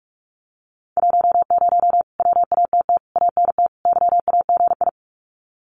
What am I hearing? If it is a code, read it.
Morse code, heard as a keyed tone, about 44 wpm, 700 Hz. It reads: 10 WATT ANT YAGI
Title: Common QSO Phrases Encore 45wpm